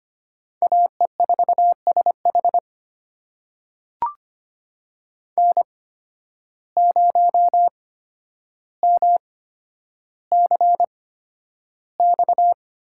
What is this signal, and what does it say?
Morse code, heard as a keyed tone, about 25 wpm, spaced as for 11 wpm, 700 Hz. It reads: AE4SH N0MCX
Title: USA Call Signs - 5 Characters - 8x Character Spacing - 25wpm